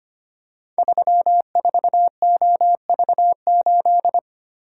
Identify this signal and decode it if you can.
Morse code, heard as a keyed tone, about 25 wpm, 700 Hz. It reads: 34OV8
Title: Code Groups - 5 Letter:Numbers - 25wpm